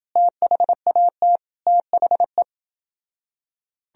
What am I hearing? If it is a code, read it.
Morse code, heard as a keyed tone, about 30 wpm, 700 Hz. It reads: THAT THE